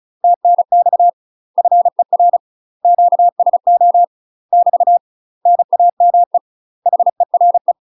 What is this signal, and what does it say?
Morse code, heard as a keyed tone, about 36 wpm, 700 Hz. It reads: TNX FER QSO <BT> NAME HERE
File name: UK MASTER QSOs TRAINING / 35WPM / UK MASTER QSOs 0018